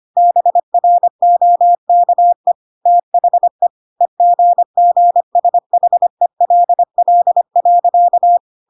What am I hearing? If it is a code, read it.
Morse code, heard as a keyed tone, about 25 wpm, 700 Hz. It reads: BROKE THE EGGSHELL.
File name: Pinocchio / Pinocchio ch5 25 25 ik1hge